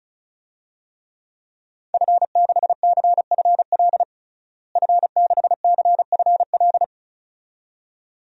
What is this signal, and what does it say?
Morse code, heard as a keyed tone, about 34 wpm, 700 Hz. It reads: F6CFL F6CFL
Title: DXCC-100 Call Signs - 5 Characters - Repeated 2x - 35wpm